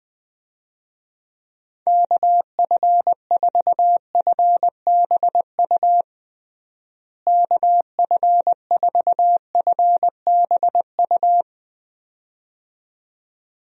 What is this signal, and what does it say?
Morse code, heard as a keyed tone, about 20 wpm, 700 Hz. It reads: KF4FBU KF4FBU
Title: USA Call Signs - 6 Characters - Repeated 2x - 20wpm